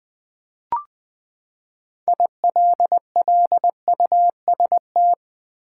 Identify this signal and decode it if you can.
Morse code, heard as a keyed tone, about 20 wpm, 700 Hz. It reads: ILLUST
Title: Build A Word - 6 Letters At A Time - 12-Letter Words - 20wpm